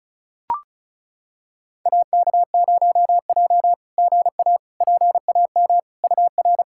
Morse code, heard as a keyed tone, about 36 wpm, 690 Hz. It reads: AK0J GA PAM UR
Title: SOTA QSO Practice 35wpm